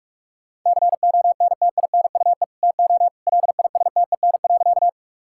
Morse code, heard as a keyed tone, about 41 wpm, 700 Hz. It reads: CONTINUE TO LISTEN.
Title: Sentences from Top 1000 Words - Review Part 6 - 45wpm